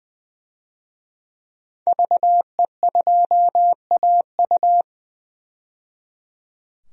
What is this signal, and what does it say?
Morse code, heard as a keyed tone, about 20 wpm, 700 Hz. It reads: VE2AU
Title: DXCC-100 Call Signs - 5 Characters - 20wpm